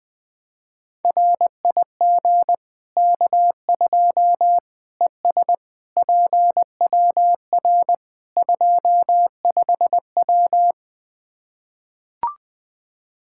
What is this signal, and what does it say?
Morse code, heard as a keyed tone, about 20 wpm, 700 Hz. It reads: RIG K2 ES PWR 25W
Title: Practice QSO - Exchange 2 - Info - 20wpm